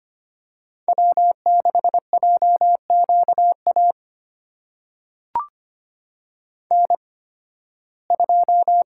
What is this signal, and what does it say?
Morse code, heard as a keyed tone, about 25 wpm, spaced as for 16 wpm, 700 Hz. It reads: W6JQA N2
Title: USA Call Signs - 5 Characters - 8x Character Spacing - 25wpm